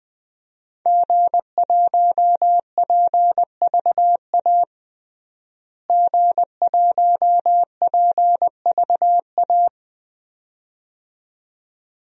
Morse code, heard as a keyed tone, about 20 wpm, 700 Hz. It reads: G1PVA G1PVA
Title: DXCC-100 Call Signs - 5 Characters - Repeated 2x - 20wpm